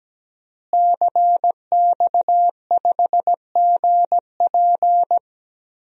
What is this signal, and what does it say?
Morse code, heard as a keyed tone, about 17 wpm, 700 Hz. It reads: CX5GP
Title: DXCC-100 Call Signs - 5 Characters - Repeated 2x - 17wpm